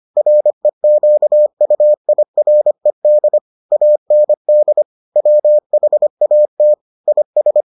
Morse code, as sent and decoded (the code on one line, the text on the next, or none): .-. --.- ..- .. .-. -.. / .- -. -.. / .-- .... .- - / .. ...
REQUIRED AND WHAT IS